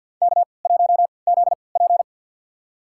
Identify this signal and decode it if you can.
Morse code, heard as a keyed tone, about 49 wpm, spaced as for 28 wpm, 700 Hz. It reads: K1ZP
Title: USA Call Signs - 4 Characters - 3x Character Spacing - 50wpm